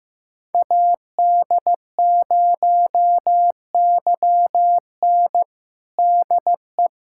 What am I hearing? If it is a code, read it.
Morse code, heard as a keyed tone, about 15 wpm, 700 Hz. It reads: AD0YN DE
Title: USA Callsign DE Callsign 15wpm